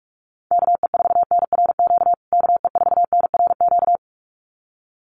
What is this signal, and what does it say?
Morse code, heard as a keyed tone, about 44 wpm, 700 Hz. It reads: KE4NRQ KE4NRQ
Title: USA Call Signs Repeated 2x 45wpm